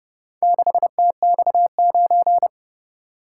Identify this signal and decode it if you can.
Morse code, heard as a keyed tone, about 30 wpm, 700 Hz. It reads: 6TX9